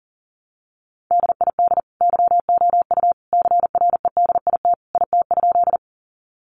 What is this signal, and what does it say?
Morse code, heard as a keyed tone, about 40 wpm, 700 Hz. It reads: DID YOU CREDIT IT?
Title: Sentences from Top 1000 Words 40wpm